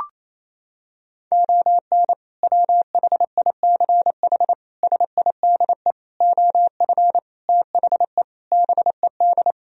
Morse code, heard as a keyed tone, about 28 wpm, 700 Hz. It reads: ON WHICH SIDE OF THE BED
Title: Sentences from Top 1000 Words - Review Part 1 - 28wpm